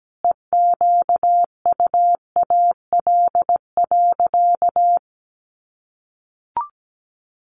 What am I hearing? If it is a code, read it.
Morse code, heard as a keyed tone, about 17 wpm, 700 Hz. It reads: EQUAL.
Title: Sentences from Top 1400 Words 17wpm